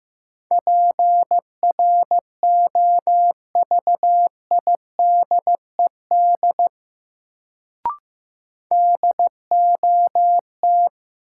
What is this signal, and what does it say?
Morse code, heard as a keyed tone, about 15 wpm, 700 Hz. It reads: PROVIDED DOT
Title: Sentences from Top 1500 Words - New Word Drill Alt - 15wpm